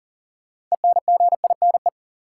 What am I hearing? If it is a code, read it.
Morse code, heard as a keyed tone, about 38 wpm, 700 Hz. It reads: ENGINE
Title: Sentences from Top 1500 Words - New Word Drill Alt - 40wpm